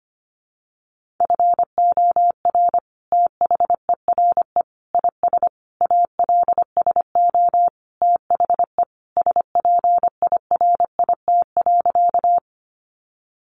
Morse code, heard as a keyed tone, about 25 wpm, 700 Hz. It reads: FOR THERE IS ALSO THE SPIRIT.
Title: Sentences from Top 1400 Words 25wpm